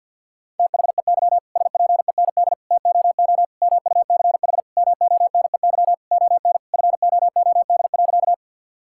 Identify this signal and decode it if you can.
Morse code, heard as a keyed tone, about 50 wpm, 700 Hz. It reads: THEY SPEND TOO MUCH MONEY ON FOOD.
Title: Sentences from Top 1000 Words - Review Part 5 - 50wpm